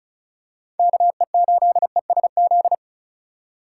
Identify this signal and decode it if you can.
Morse code, heard as a keyed tone, about 34 wpm, 700 Hz. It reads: KE8ESZ
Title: USA Call Signs - 6 Characters - 4x Character Spacing - 35wpm